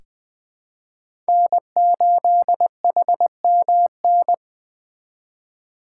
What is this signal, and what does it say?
Morse code, heard as a keyed tone, about 20 wpm, 700 Hz. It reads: N8HMN
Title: USA Call Signs - 5 Characters - 3x Character Spacing - 20wpm